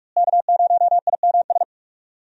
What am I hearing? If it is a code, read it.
Morse code, heard as a keyed tone, about 45 wpm, 700 Hz. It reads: K0IMS
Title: USA Call Signs - 5 Characters - Repeated 2x - 45wpm